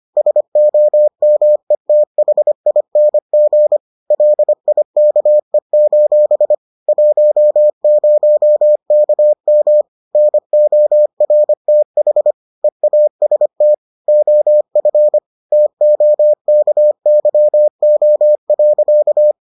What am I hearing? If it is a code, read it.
Morse code, heard as a keyed tone, about 25 wpm, 600 Hz. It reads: SOMETHING LIKE: 10KM NORTH EAST OF TOKYO.